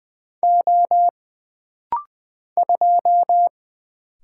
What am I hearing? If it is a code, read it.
Morse code, heard as a keyed tone, about 20 wpm, 700 Hz. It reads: O 2
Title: Single Letter-Number Alt - Rapid-Fire 20wpm